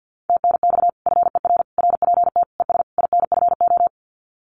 Morse code, heard as a keyed tone, about 50 wpm, 700 Hz. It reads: TNX FER RPT ES INFO
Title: Common QSO Phrases Encore 50wpm